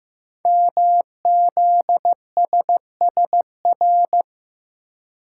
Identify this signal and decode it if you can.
Morse code, heard as a keyed tone, about 15 wpm, 700 Hz. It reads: MZSSR
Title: Code Groups - 5 Letters - 15wpm